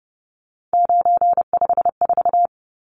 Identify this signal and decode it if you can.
Morse code, heard as a keyed tone, about 30 wpm, 700 Hz. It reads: 954
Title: Numbers - 3-Digit 30wpm